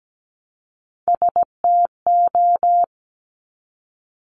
Morse code, heard as a keyed tone, about 17 wpm, 700 Hz. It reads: STO
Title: Top Three Letter Combinations in Words 17wpm